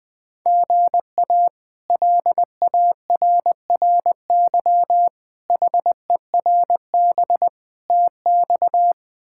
Morse code, heard as a keyed tone, about 20 wpm, 700 Hz. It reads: GA LARRY HERB TX
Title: K1USN SST QSO Practice - 20wpm